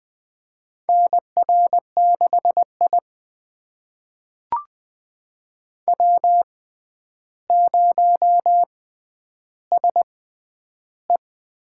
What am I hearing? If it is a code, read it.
Morse code, heard as a keyed tone, about 20 wpm, spaced as for 10 wpm, 700 Hz. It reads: NR6I W0SE
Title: USA Call Signs - 4 Characters - 6x Character Spacing - 20wpm